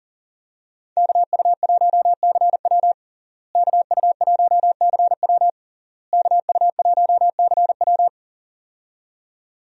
Morse code, heard as a keyed tone, about 37 wpm, 700 Hz. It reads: KU1CW KU1CW KU1CW
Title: USA Call Signs - 5 Characters - Repeated 3x - 40wpm